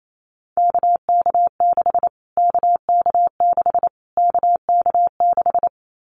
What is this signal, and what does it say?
Morse code, heard as a keyed tone, about 28 wpm, 700 Hz. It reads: KK6 KK6 KK6
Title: Callsign Prefixes for Top 25 DXCC - Repeated 3x - 28wpm